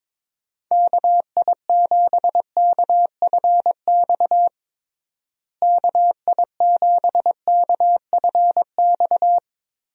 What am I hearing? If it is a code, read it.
Morse code, heard as a keyed tone, about 21 wpm, 700 Hz. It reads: KI7KFX KI7KFX
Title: USA Call Signs - 6 Characters - Repeated 3x - 22wpm